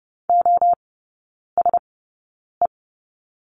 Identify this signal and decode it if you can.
Morse code, heard as a keyed tone, about 30 wpm, spaced as for 9 wpm, 700 Hz. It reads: OSE
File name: 081 Lesson 14e - Number 2 - Three Characters 30wpm